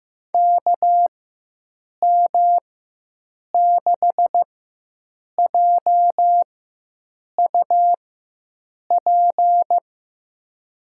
Morse code, heard as a keyed tone, about 15 wpm, spaced as for 7 wpm, 700 Hz. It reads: KM6JUP
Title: USA Call Signs - 6 Characters - 4x Character Spacing - 15wpm